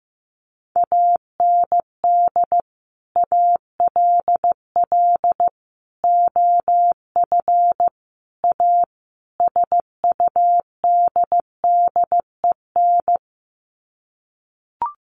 Morse code, heard as a keyed tone, about 15 wpm, 700 Hz. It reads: AND ALL OF A SUDDEN
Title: Sets of 5 Words 15wpm